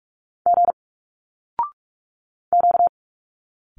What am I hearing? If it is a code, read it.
Morse code, heard as a keyed tone, about 46 wpm, 700 Hz. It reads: G Q
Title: Single Letters 45wpm